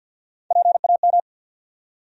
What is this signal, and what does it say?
Morse code, heard as a keyed tone, about 51 wpm, 700 Hz. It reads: PAM
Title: USA Common Names - Max 3 Letters - 50wpm